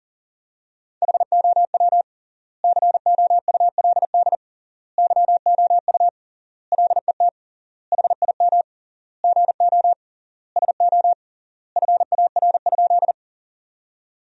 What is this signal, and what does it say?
Morse code, heard as a keyed tone, about 40 wpm, 700 Hz. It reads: HOW COULD YOU LET HIM GO SO FAR?
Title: 3x Word-Spacing - Sentences from Top 300 Words - 40wpm